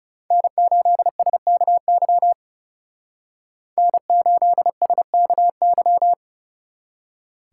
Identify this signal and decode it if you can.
Morse code, heard as a keyed tone, about 38 wpm, 690 Hz. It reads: N8SKY N8SKY